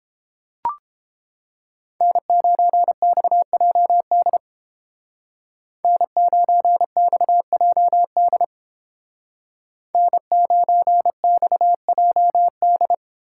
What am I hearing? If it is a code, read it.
Morse code, heard as a keyed tone, about 29 wpm, 700 Hz. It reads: N9XJD N9XJD N9XJD